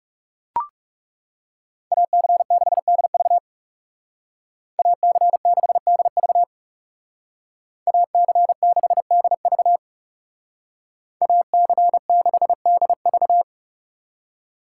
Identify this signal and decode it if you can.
Morse code, heard as a keyed tone, about 37 wpm, 700 Hz. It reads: AC6DV AC6DV AC6DV AC6DV
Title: Speed Racing - USA 2x2 Call Signs 30wpm